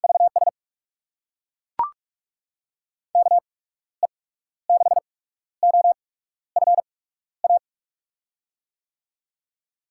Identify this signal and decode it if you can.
Morse code, heard as a keyed tone, about 42 wpm, spaced as for 14 wpm, 700 Hz. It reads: VS KE6OFA
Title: USA Call Signs - 6 Characters - 8x Character Spacing - 45wpm